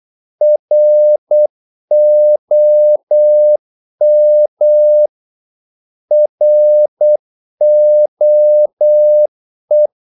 Morse code, heard as a keyed tone, about 8 wpm, 600 Hz. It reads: ROM ROE
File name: Korte ord DK 8wpm